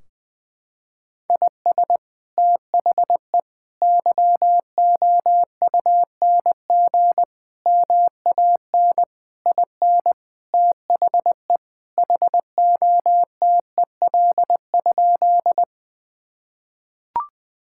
Morse code, heard as a keyed tone, about 20 wpm, 700 Hz. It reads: IS THE YOUNG MAN IN THE HOTEL?
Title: Sentences from Top 1600 Words 20wpm